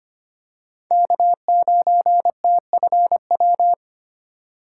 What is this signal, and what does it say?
Morse code, heard as a keyed tone, about 25 wpm, 700 Hz. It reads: K9TFW
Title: USA Call Signs - 5 Characters - 4x Character Spacing - 25wpm